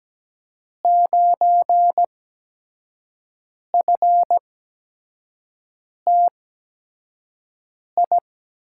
Morse code, heard as a keyed tone, about 17 wpm, spaced as for 5 wpm, 700 Hz. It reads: 9FTI
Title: USA Call Signs - 5 Characters - 8x Character Spacing - 17wpm